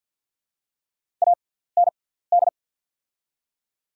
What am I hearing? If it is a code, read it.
Morse code, heard as a keyed tone, about 48 wpm, spaced as for 17 wpm, 700 Hz. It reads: AND
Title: 6x Character Spacing - Top 100 Words - 50wpm